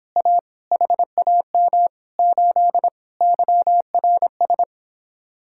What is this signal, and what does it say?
Morse code, heard as a keyed tone, about 26 wpm, 700 Hz. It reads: A HAM 8 YRS